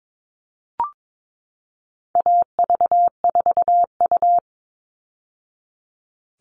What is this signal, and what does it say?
Morse code, heard as a keyed tone, about 22 wpm, 700 Hz. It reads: AV4U
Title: All Cut Numbers - 4-Digit 22wpm